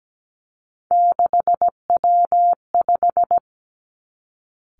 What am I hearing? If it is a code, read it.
Morse code, heard as a keyed tone, about 17 wpm, 700 Hz. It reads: 6W5
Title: Three Challenging Letter-Number Combinations 17wpm